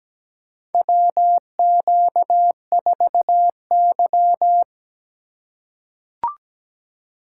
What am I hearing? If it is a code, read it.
Morse code, heard as a keyed tone, about 17 wpm, 700 Hz. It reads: WQ4Y
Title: USA Call Signs - 3x Character Spacing - 17wpm